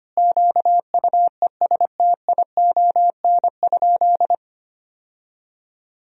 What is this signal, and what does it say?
Morse code, heard as a keyed tone, about 25 wpm, 700 Hz. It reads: QUESTION?
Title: 4x Word-Spacing - Sentences from Top 400 Words - 25wpm